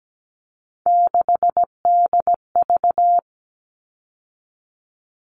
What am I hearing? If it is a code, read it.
Morse code, heard as a keyed tone, about 17 wpm, 700 Hz. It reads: 6DV
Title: All Cut Numbers - 3-Digit 17wpm